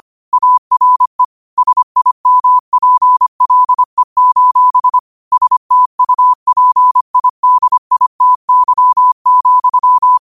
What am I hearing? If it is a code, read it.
Morse code, heard as a keyed tone, about 25 wpm, 1000 Hz. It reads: ARE SIMPLE: STUPIDITY,